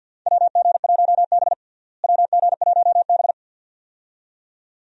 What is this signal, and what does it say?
Morse code, heard as a keyed tone, about 49 wpm, 700 Hz. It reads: WG1B WG1B
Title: USA Call Signs - 4 Characters - Repeated 2x - 50wpm